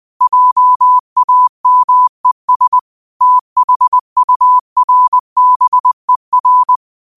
Morse code, heard as a keyed tone, about 20 wpm, 1000 Hz. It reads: JAMES THURBER
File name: qotdfiles / 20wpm